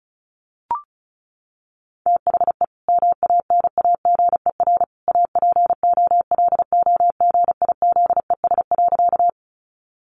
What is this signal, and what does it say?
Morse code, heard as a keyed tone, about 34 wpm, 700 Hz. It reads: THE MANAGER APOLOGIZES.
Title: Additional Easy Phrases 35wpm